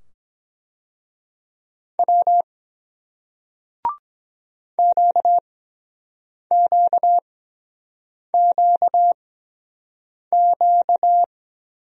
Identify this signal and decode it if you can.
Morse code, heard as a keyed tone, about 24 wpm, spaced as for 7 wpm, 700 Hz. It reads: W QQQQ